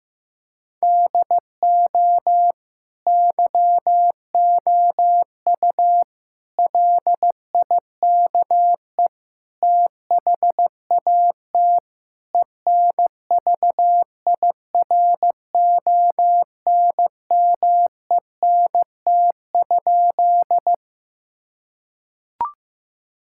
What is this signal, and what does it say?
Morse code, heard as a keyed tone, about 15 wpm, 700 Hz. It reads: DO YOU LIKE THAT ENVIRONMENT?
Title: Sentences from Top 1000 Words - Review Part 5 - 15wpm